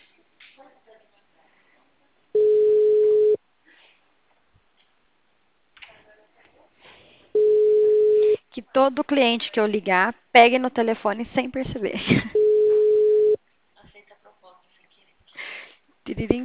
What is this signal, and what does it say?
Ringback tone (Europe, 425 Hz), with speech.